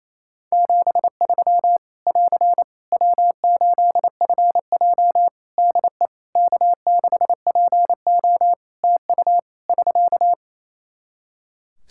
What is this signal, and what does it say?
Morse code, heard as a keyed tone, about 27 wpm, 700 Hz. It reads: 73 <AR> W8FJ DE K6PO TU <SK>
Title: Practice QSO - Exchange 4 - Ending - 28wpm